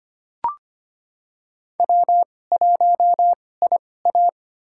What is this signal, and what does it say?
Morse code, heard as a keyed tone, about 25 wpm, spaced as for 18 wpm, 700 Hz. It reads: W1IA
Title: USA Call Signs - 4 Characters - 2x Character Spacing - 25wpm